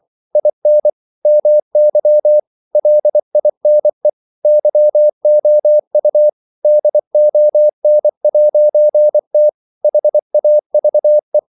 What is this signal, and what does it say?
Morse code, heard as a keyed tone, about 24 wpm, 600 Hz. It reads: IN MY LINE YOU DON'T HAVE
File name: CW / Practice / EnglishQuotations0324